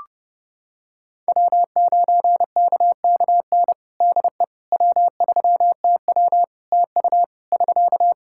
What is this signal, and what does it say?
Morse code, heard as a keyed tone, about 30 wpm, 700 Hz. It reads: W9KKN DE W3TW TU <SK>
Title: Practice QSO - Exchange 4 - Ending - 30wpm